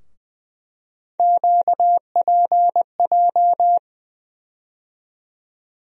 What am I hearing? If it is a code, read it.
Morse code, heard as a keyed tone, about 20 wpm, 700 Hz. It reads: QPJ
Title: USA Call Sign Postfixes - 3 Letters 20wpm